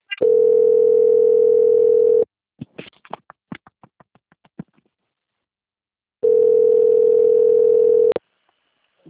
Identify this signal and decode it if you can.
Ringback tone (North America), 440 + 480 Hz.